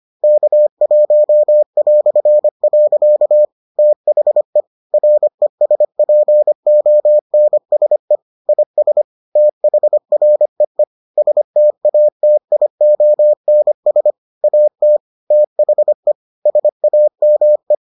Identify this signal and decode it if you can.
Morse code, heard as a keyed tone, about 25 wpm, 600 Hz. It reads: K1". THE RESPONSE IS THREE STATIONS AT THE SAME